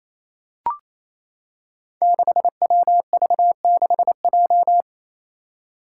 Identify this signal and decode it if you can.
Morse code, heard as a keyed tone, about 28 wpm, 700 Hz. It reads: 6WV6J
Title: Five Challenging Letter-Number Combinations 28wpm